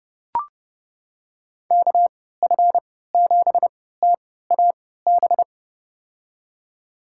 Morse code, heard as a keyed tone, about 30 wpm, spaced as for 17 wpm, 700 Hz. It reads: KF7TAB
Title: USA Call Signs - 6 Characters - 3x Character Spacing - 30wpm